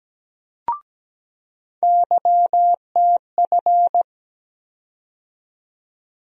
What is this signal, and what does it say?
Morse code, heard as a keyed tone, about 17 wpm, 700 Hz. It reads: YTF
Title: USA Call Sign Postfixes - 3 Letters 17wpm